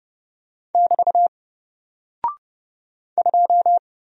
Morse code, heard as a keyed tone, about 30 wpm, 700 Hz. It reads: <BT> 2
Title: Single Character (Letter:Number:Punctuation:Prosign) Alt 30wpm